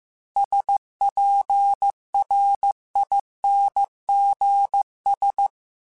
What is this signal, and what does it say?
Morse code, heard as a keyed tone, about 15 wpm, 800 Hz. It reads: SPRINGS
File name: fromSMs / FromSMs mp3s / Cities@13